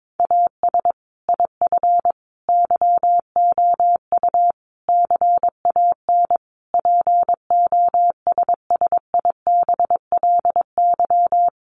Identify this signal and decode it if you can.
Morse code, heard as a keyed tone, about 22 wpm, 700 Hz. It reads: AS IF YOU CAN POSSIBLY
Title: Sentences from Top 1300 Words 22wpm